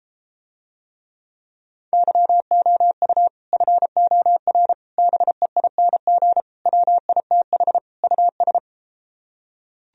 Morse code, heard as a keyed tone, about 33 wpm, 700 Hz. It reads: YOU FOR BEING WITH US